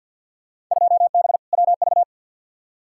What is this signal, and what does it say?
Morse code, heard as a keyed tone, about 52 wpm, 700 Hz. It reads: 2B WV
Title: Field Day Exchanges Encore - 50wpm